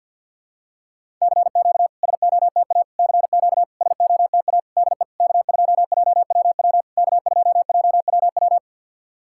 Morse code, heard as a keyed tone, about 50 wpm, 700 Hz. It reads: CQ SOTA CQ SOTA DE K2JWW K2JWW
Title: SOTA QSO Practice 50wpm